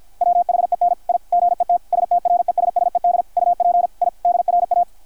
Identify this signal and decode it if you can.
Morse code, heard as a keyed tone, about 53 wpm, 720 Hz, with radio noise. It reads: WHEN I GET STRESSED UP I DRA
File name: archived / CRSnet